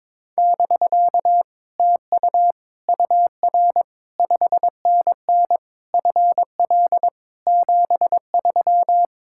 Morse code, heard as a keyed tone, about 22 wpm, 700 Hz. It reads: <BK> TU UR 5NN FL 73
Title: Speed-Racing Common QSO Phrases Encore - 22wpm